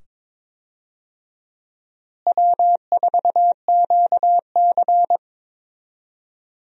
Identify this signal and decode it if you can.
Morse code, heard as a keyed tone, about 22 wpm, 700 Hz. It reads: W4QC
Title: USA Call Signs - 4 Characters - Repeated 2x - 22wpm